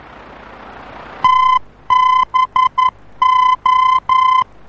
Morse code, heard as a keyed tone, about 11 wpm, 1020 Hz. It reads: TBO
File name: france / mx / TBO-F-VOR